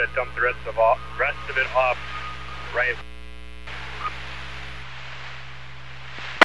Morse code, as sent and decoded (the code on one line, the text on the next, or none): none